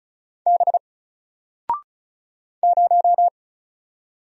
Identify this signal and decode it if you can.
Morse code, heard as a keyed tone, about 36 wpm, 700 Hz. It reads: B 0
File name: Single-Letter-Number-35wpm